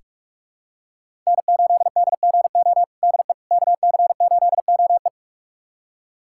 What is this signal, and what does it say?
Morse code, heard as a keyed tone, about 46 wpm, 700 Hz. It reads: N8DGO DE KC8OE
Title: USA Callsign DE Callsign - 5 Characters - 45wpm